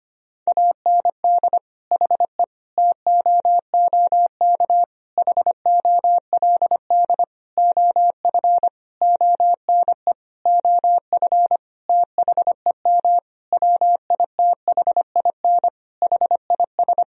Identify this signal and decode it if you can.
Morse code, heard as a keyed tone, about 25 wpm, 700 Hz. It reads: AND HE TOOK HOLD OF ONE OF THEM WITHIN HIS
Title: Sentences from Top 1000 Words - Review Part 4 - 25wpm